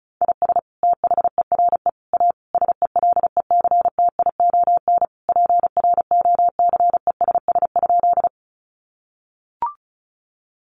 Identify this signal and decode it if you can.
Morse code, heard as a keyed tone, about 34 wpm, 700 Hz. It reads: IS THERE A SELECTION PROCESS?
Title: Sentences from Top 1400 Words 35wpm